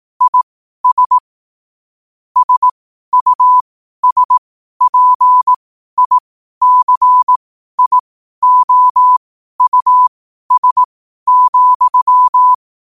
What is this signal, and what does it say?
Morse code, heard as a keyed tone, about 18 wpm, spaced as for 12 wpm, 1000 Hz. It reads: IS SUSPICIOUS,
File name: qotdfiles / 12wpm